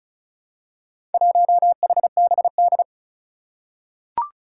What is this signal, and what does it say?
Morse code, heard as a keyed tone, about 36 wpm, 700 Hz. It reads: 1HBD
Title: Four Challenging Letter-Number Combinations 35wpm